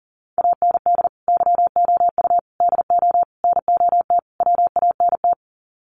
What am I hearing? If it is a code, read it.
Morse code, heard as a keyed tone, about 40 wpm, 700 Hz. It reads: AND YOU DO NOT WANT
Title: Sets of 5 Words Encore2 40wpm